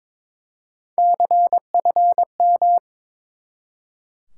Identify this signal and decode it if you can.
Morse code, heard as a keyed tone, about 21 wpm, 700 Hz. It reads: CFM
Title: Top 90 QSO Elements 22wpm